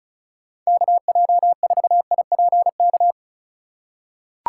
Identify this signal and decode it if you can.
Morse code, heard as a keyed tone, about 34 wpm, 700 Hz. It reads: KJ4IPK
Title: USA Call Signs - 6 Characters - 4x Character Spacing - 35wpm